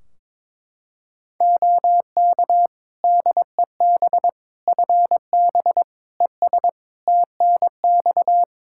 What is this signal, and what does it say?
Morse code, heard as a keyed tone, about 22 wpm, 700 Hz. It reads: OK DEB FB ES TNX
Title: Practice QSO - Exchange 2 - Info - 22wpm